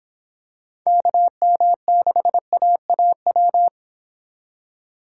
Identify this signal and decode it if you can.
Morse code, heard as a keyed tone, about 26 wpm, 700 Hz. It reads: KM6AAW